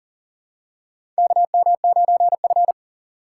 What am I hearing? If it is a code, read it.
Morse code, heard as a keyed tone, about 40 wpm, 700 Hz. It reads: KM9F
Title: USA Call Signs - 4 Characters - 6x Character Spacing - 40wpm